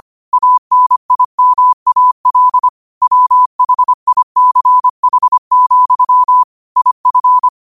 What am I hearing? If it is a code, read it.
Morse code, heard as a keyed tone, about 25 wpm, 1000 Hz. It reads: ANIMAL WHICH, IF
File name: qotdfiles / 25wpm